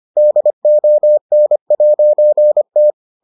Morse code, heard as a keyed tone, about 25 wpm, 600 Hz. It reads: DON'T